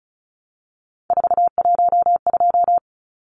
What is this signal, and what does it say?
Morse code, heard as a keyed tone, about 34 wpm, 700 Hz. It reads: 412
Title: Numbers - 3-Digit 35wpm